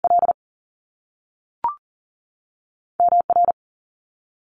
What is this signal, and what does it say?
Morse code, heard as a keyed tone, about 39 wpm, 700 Hz. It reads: L M R
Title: Top Two Letter Combinations in Words 40wpm